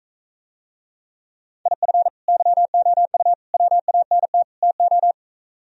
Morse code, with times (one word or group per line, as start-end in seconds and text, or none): 1.7-2.1 IF
2.3-3.3 YOU
3.5-4.4 WANT
4.6-5.1 TO